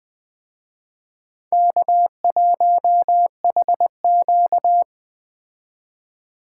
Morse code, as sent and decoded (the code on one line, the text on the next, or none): -.- .---- .... --.-
K1HQ